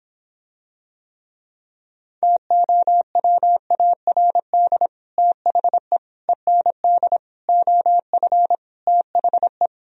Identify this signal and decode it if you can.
Morse code, heard as a keyed tone, about 26 wpm, 700 Hz. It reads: TOWARD THE END OF THE